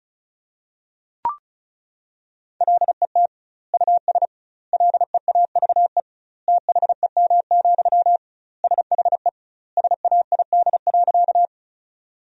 Morse code, heard as a keyed tone, about 34 wpm, 700 Hz. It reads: LET US LEAVE THEM, SHE SAID.
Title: 2x Word-Spacing - Sentences from Top 400 Words - 35wpm